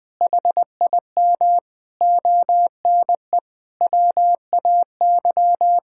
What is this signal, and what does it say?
Morse code, heard as a keyed tone, about 20 wpm, 700 Hz. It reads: HIM ONE WAY
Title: Sentences from Top 1000 Words - Review Part 6 - 20wpm